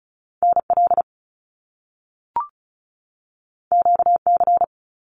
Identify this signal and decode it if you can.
Morse code, heard as a keyed tone, about 36 wpm, 700 Hz. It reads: NL QC
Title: Canadian Provinces and Territories 35wpm